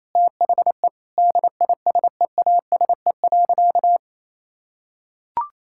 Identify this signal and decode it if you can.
Morse code, heard as a keyed tone, about 27 wpm, 690 Hz. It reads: THE DISEASE.
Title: Sentences from Top 1000 Words - Review Part 6 - 28wpm